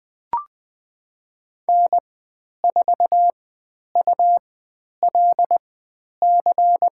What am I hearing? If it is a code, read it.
Morse code, heard as a keyed tone, about 20 wpm, spaced as for 10 wpm, 700 Hz. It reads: N4ULC
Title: 102 Lesson 17g - Letter u - Call Signs 20wpm